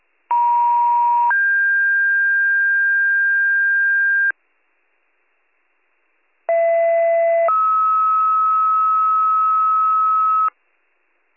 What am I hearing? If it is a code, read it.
Two-tone page, heard as a one-second tone, then a three-second tone, about 950 then 1600 Hz, about 670 then 1290 Hz. It is heard from 0.3 to 10.5 s.